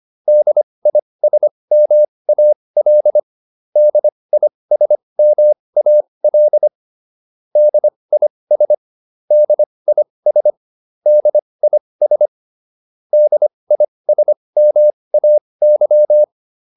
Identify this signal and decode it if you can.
Morse code, heard as a keyed tone, about 25 wpm, spaced as for 20 wpm, 600 Hz. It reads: DISMAL DISMAL DIS DIS DIS DISMAY